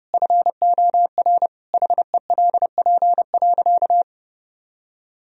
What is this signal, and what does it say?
Morse code, heard as a keyed tone, about 30 wpm, 700 Hz. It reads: FOR HELP.
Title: Sentences from Top 1000 Words - Review Part 3 - 30wpm